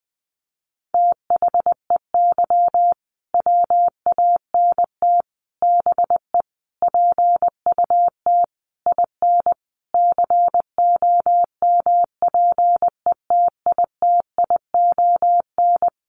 Morse code, heard as a keyed tone, about 20 wpm, 700 Hz. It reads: THEY WANT BE PUT IN COMPETITION
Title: Sentences from Top 1000 Words 20wpm